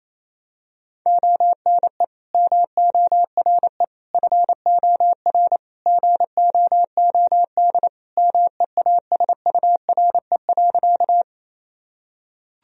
Morse code, heard as a keyed tone, about 28 wpm, 700 Hz. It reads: ONE MORE FOR GOOD MEASURE.
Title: Sentences from Top 1000 Words - Review Part 2 - 28wpm